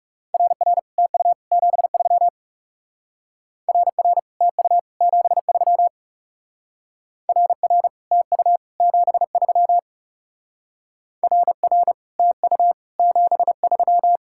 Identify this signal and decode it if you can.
Morse code, heard as a keyed tone, about 36 wpm, 700 Hz. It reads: RR TU 73 RR TU 73 RR TU 73 RR TU 73
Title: Speed-Racing Common QSO Phrases - 30wpm